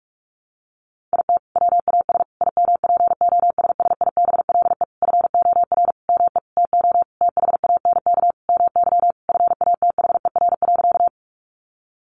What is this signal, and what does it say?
Morse code, heard as a keyed tone, about 44 wpm, 700 Hz. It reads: IT WAS IMPOSSIBLE FOR ME TO THANK MY FATHER.
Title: Sentences from Top 1100 Words 45wpm